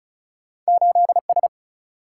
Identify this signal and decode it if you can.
Morse code, heard as a keyed tone, about 36 wpm, 700 Hz. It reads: 8 S
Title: Single-Letter Single-Number 35wpm